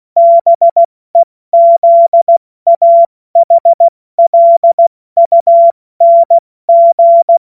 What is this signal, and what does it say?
Morse code, heard as a keyed tone, about 16 wpm, 680 Hz. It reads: BEZAHLUNG
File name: Morsekurs / Witze / Witz 11. April 70 F80